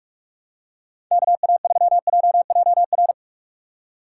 Morse code, heard as a keyed tone, about 47 wpm, 690 Hz. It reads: KA3JJR
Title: USA Call Signs - 6 Characters - 3x Character Spacing - 45wpm